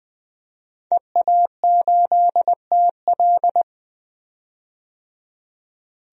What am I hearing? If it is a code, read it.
Morse code, heard as a keyed tone, about 21 wpm, 700 Hz. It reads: EA8TL
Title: DXCC-100 Call Signs - 5 Characters - 20wpm